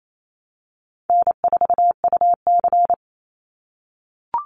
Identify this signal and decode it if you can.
Morse code, heard as a keyed tone, about 28 wpm, 700 Hz. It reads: N4UC
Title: Call Signs - 28wpm